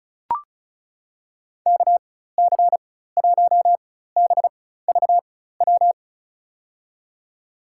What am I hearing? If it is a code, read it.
Morse code, heard as a keyed tone, about 36 wpm, spaced as for 16 wpm, 700 Hz. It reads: KC1BVW
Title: USA Call Signs - 6 Characters - 4x Character Spacing - 35wpm